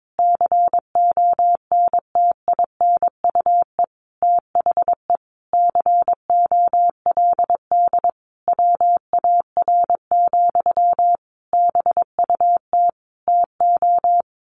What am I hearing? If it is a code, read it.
Morse code, heard as a keyed tone, about 22 wpm, 700 Hz. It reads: CONTINUE THE COLD WAR, BUT TO
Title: Sentences from Top 1200 Words 22wpm